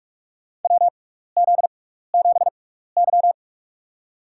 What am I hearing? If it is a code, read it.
Morse code, heard as a keyed tone, about 44 wpm, spaced as for 15 wpm, 700 Hz. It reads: WZ7Y
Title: USA Call Signs - 4 Characters - 6x Character Spacing - 45wpm